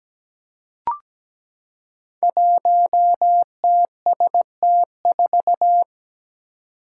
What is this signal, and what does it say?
Morse code, heard as a keyed tone, about 17 wpm, 700 Hz. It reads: 1TST4
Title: Five Challenging Letter-Number Combinations 17wpm